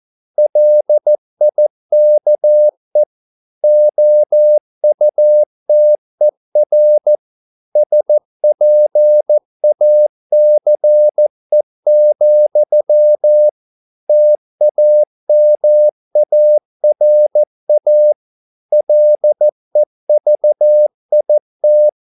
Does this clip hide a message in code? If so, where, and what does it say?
Yes: Morse code from 0.4 to 21.9 s, heard as a keyed tone, about 14 wpm, 600 Hz. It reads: LIKE OUTER SPACE, TAMARA LEVIT